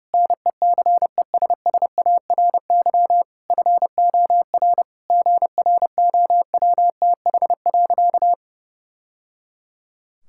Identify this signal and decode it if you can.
Morse code, heard as a keyed tone, about 32 wpm, 700 Hz. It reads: NECESSARY FOR GROWTH.
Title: Sentences from Top 1000 Words - Review Part 5 - 30wpm